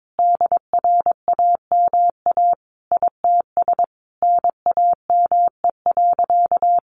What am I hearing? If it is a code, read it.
Morse code, heard as a keyed tone, about 22 wpm, 700 Hz. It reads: DRAMA ITS NAME.
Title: Sentences from Top 1500 Words 22wpm